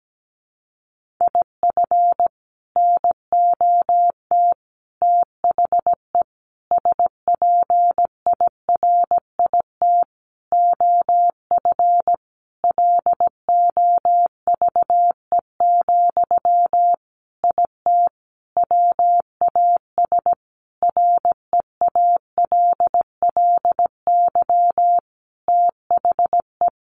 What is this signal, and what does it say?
Morse code, heard as a keyed tone, about 17 wpm, 700 Hz. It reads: IF NOT THE SPIRIT OF LOVE, IT WAS REALLY THE
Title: Sentences from Top 1400 Words 17wpm